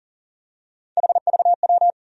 Morse code, heard as a keyed tone, about 40 wpm, 700 Hz. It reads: HVW
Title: USA Call Sign Postfixes - 3 Letters 40wpm